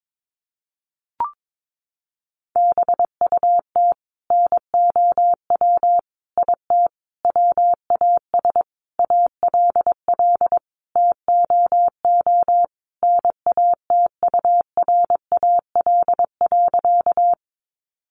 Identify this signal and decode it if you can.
Morse code, heard as a keyed tone, about 22 wpm, 700 Hz. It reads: BUT NOW IT WAS ALL TOO NATURAL.
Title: Sentences from Top 700 Words 22wpm